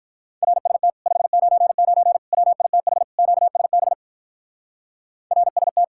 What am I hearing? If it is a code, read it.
Morse code, heard as a keyed tone, about 46 wpm, 700 Hz. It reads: RST 599 WITH QSB RST